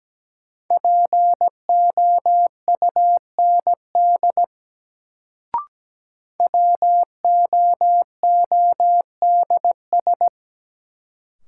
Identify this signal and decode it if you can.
Morse code, heard as a keyed tone, about 17 wpm, 700 Hz. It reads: POUND WOODS
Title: Sentences from Top 1000 Words - Review Part 2 - New Word Drill - 17wpm